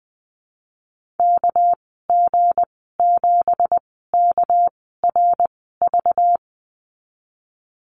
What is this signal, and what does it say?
Morse code, heard as a keyed tone, about 20 wpm, spaced as for 14 wpm, 700 Hz. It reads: KG7KRV
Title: 231 Lesson 37g - Letter x - Call Signs 20wpm